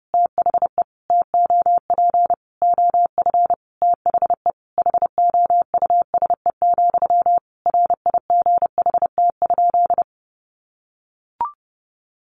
Morse code, heard as a keyed tone, about 30 wpm, 700 Hz. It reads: THE TOP OF THE HOUSE, RIGHT?
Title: Sentences from Top 1200 Words 30wpm